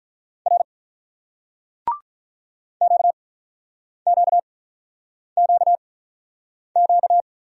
Morse code, heard as a keyed tone, about 48 wpm, spaced as for 9 wpm, 700 Hz. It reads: R QQQQ